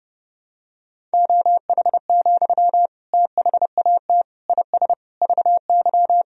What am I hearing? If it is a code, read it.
Morse code, heard as a keyed tone, about 30 wpm, 700 Hz. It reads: OH, THAT IS VY
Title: Sentences from Top 500 Words and Abbreviations 30wpm